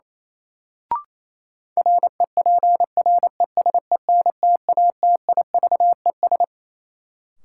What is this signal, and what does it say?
Morse code, heard as a keyed tone, about 28 wpm, 700 Hz. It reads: REPRESENTATIVES
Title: Sets of 5 Words Encore - New Word Drill - 28wpm